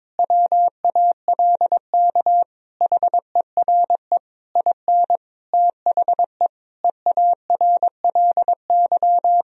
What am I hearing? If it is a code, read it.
Morse code, heard as a keyed tone, about 22 wpm, 700 Hz. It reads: WALK HERE IN THE EARLY
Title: Sentences from Top 1000 Words - Review Part 3 - 22wpm